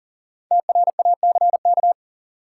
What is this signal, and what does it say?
Morse code, heard as a keyed tone, about 40 wpm, 700 Hz. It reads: TRACK
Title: Sentences from Top 1100 Words - New Word Drill - 40wpm